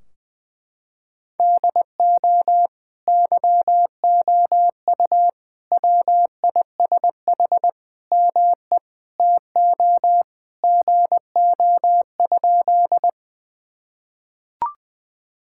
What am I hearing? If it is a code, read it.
Morse code, heard as a keyed tone, about 19 wpm, 700 Hz. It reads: DO YOU WISH ME TO GO?
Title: Sentences from Top 1600 Words 20wpm